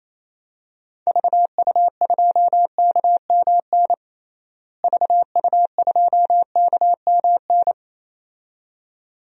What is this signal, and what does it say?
Morse code, heard as a keyed tone, about 28 wpm, 700 Hz. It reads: VU2KMN VU2KMN
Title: DXCC-100 Call Signs - 6 Characters - Repeated 2x - 28wpm